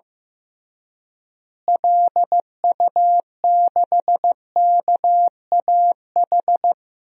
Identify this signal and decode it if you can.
Morse code, heard as a keyed tone, about 16 wpm, 700 Hz. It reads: LU6KAH
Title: DXCC-100 Call Signs - 6 Characters - Repeated 2x - 15wpm